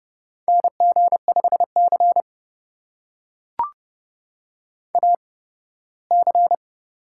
Morse code, heard as a keyed tone, about 30 wpm, spaced as for 18 wpm, 700 Hz. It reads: NG5C AC
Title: USA Call Signs - 4 Characters - 8x Character Spacing - 30wpm